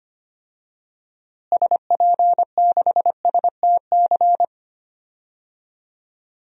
Morse code, heard as a keyed tone, about 25 wpm, 700 Hz. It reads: SP6STC
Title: DXCC-100 Call Signs - 6 Characters - Repeated 3x - 25wpm